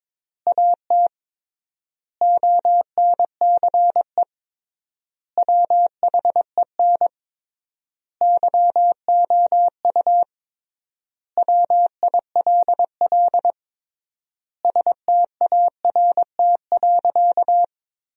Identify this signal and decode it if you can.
Morse code, heard as a keyed tone, about 23 wpm, 700 Hz. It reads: AT ONCE WHEN YOU WILL START.
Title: 3x Word-Spacing - Sentences from Top 300 Words - 22wpm